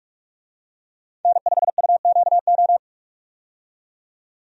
Morse code, heard as a keyed tone, about 46 wpm, 700 Hz. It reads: N5UQO
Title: USA Call Signs - 5 Characters - 45wpm